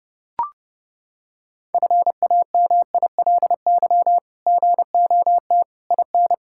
Morse code, heard as a keyed tone, about 30 wpm, 700 Hz. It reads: FAMILY GOT IN
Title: Sentences from Top 1000 Words - Review Part 6 - 30wpm